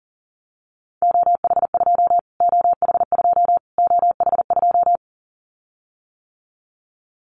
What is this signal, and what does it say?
Morse code, heard as a keyed tone, about 37 wpm, 700 Hz. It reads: OH2 OH2 OH2
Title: Callsign Prefixes for Top 25 DXCC - Repeated 3x - 40wpm